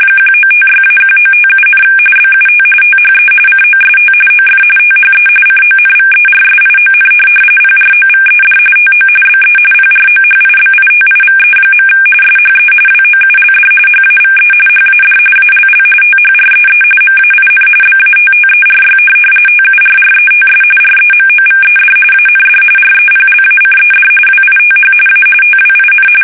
RTTY, about 45.45 baud. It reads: 1956-1957, BUT THE COST OF THE CONVERSION PROVED TO BE PROHIBITIVE. SHE WOULD HAVE HAD ALL THREE MAIN BATTERY TURRETS REMOVED AND REPLACED WITH A TWIN
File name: fsk850 / USS-Massachusetts